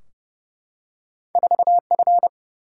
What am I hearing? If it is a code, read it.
Morse code, heard as a keyed tone, about 29 wpm, 700 Hz. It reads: 4F
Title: Single-Letter Single-Number 30wpm